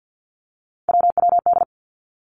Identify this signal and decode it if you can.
Morse code, heard as a keyed tone, about 50 wpm, 700 Hz. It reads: WWD
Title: Three Challenging Letter-Number Combinations 50wpm